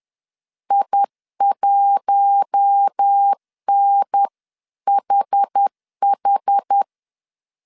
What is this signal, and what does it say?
Morse code, heard as a keyed tone, about 10 wpm, 800 Hz. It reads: I1NHH